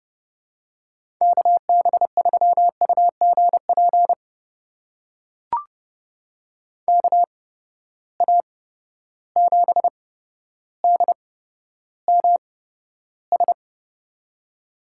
Morse code, heard as a keyed tone, about 31 wpm, spaced as for 13 wpm, 700 Hz. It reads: KB3UGP KA7DMS
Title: USA Call Signs - 8x Character Spacing - 30wpm